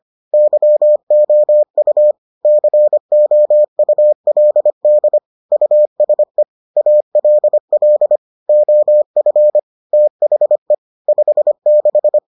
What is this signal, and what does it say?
Morse code, heard as a keyed tone, about 25 wpm, 600 Hz. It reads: YOU COULD USE ALL OF THE 56